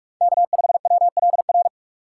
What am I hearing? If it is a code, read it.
Morse code, heard as a keyed tone, about 43 wpm, 700 Hz. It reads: K5WLR
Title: USA Call Signs - 4x Character Spacing - 45wpm